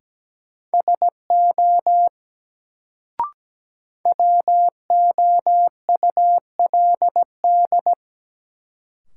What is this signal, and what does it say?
Morse code, heard as a keyed tone, about 17 wpm, 700 Hz. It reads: SO WOULD
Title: Sets of 2 Words - New Word Drill - 17wpm